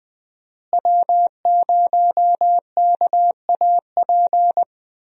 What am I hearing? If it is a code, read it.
Morse code, heard as a keyed tone, about 20 wpm, 700 Hz. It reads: W0KAP
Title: USA Call Signs - 5 Characters - Repeated 3x - 20wpm